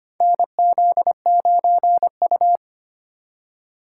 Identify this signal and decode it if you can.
Morse code, heard as a keyed tone, about 25 wpm, 700 Hz. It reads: NZ9U
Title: USA Call Signs - 4 Characters - 2x Character Spacing - 25wpm